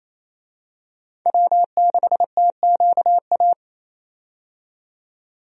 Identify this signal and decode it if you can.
Morse code, heard as a keyed tone, about 28 wpm, 700 Hz. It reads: W6TQA
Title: USA Call Signs - 5 Characters - 28wpm